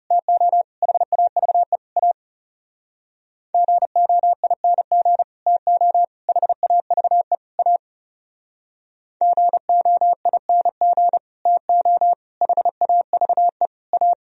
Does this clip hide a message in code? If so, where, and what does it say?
Yes: Morse code from 0.1 to 14.1 s, heard as a keyed tone, about 38 wpm, 700 Hz. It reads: TO HAVE A GOING TO HAVE A GOING TO HAVE A